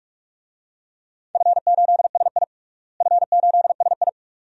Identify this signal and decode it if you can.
Morse code, heard as a keyed tone, about 46 wpm, 700 Hz. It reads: F8SI F8SI
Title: DXCC-100 Call Signs - 4 Characters - Repeated 2x - 45wpm